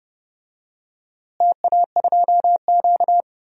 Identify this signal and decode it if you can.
Morse code, heard as a keyed tone, about 30 wpm, 700 Hz. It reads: TA2Q
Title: DXCC-100 Call Signs - 4 Characters - Repeated 2x - 30wpm